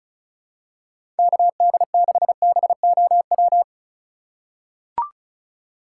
Morse code, heard as a keyed tone, about 36 wpm, 700 Hz. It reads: KD6BOW
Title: USA Call Signs - 6 Characters - 3x Character Spacing - 35wpm